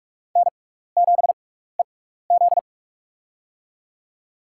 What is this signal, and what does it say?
Morse code, heard as a keyed tone, about 47 wpm, spaced as for 16 wpm, 700 Hz. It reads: N7EZ
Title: USA Call Signs - 4 Characters - 6x Character Spacing - 45wpm